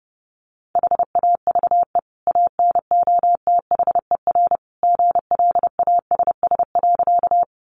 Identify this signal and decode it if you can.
Morse code, heard as a keyed tone, about 30 wpm, 700 Hz. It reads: HAVE ANOTHER GLASS.
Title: Sentences from Top 1300 Words 30wpm